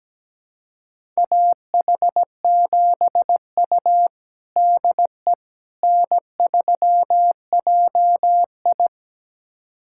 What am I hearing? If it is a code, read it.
Morse code, heard as a keyed tone, about 17 wpm, 700 Hz. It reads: AH7U DE N3JI
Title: USA Callsign DE Callsign - 4 Characters - 17wpm